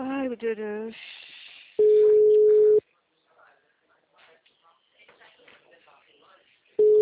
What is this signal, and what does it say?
Ringback tone (Europe, 425 Hz), with speech.